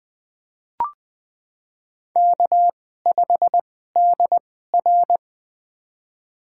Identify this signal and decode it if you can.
Morse code, heard as a keyed tone, about 20 wpm, spaced as for 14 wpm, 700 Hz. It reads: K5DR
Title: USA Call Signs - 4 Characters - 2x Character Spacing - 20wpm